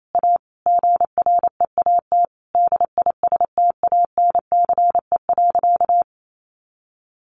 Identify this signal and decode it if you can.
Morse code, heard as a keyed tone, about 28 wpm, 700 Hz. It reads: A GREAT DISTANCE.
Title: Sentences from Top 1200 Words 28wpm